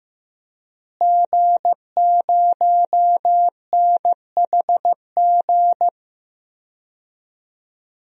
Morse code, heard as a keyed tone, about 16 wpm, 700 Hz. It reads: G0NHG
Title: DXCC-100 Call Signs - 5 Characters - Repeated 2x - 15wpm